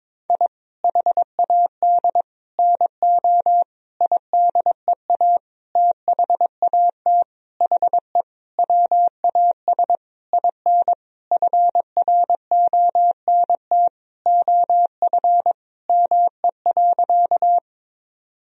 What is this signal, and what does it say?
Morse code, heard as a keyed tone, about 22 wpm, 700 Hz. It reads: I HAD NO IDEA THAT HE WAS IN FRONT OF ME.
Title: Sentences from Top 1000 Words - Review Part 2 - 22wpm